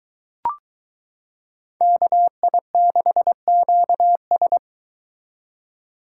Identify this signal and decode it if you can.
Morse code, heard as a keyed tone, about 23 wpm, 700 Hz. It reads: KI6QS